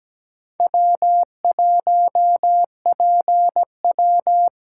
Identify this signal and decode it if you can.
Morse code, heard as a keyed tone, about 17 wpm, 700 Hz. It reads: W 1 P W
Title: USA Call Signs - 5 Characters - 8x Character Spacing - 17wpm